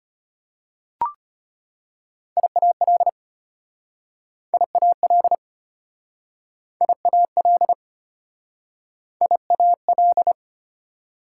Morse code, heard as a keyed tone, about 33 wpm, 690 Hz. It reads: IAL IAL IAL IAL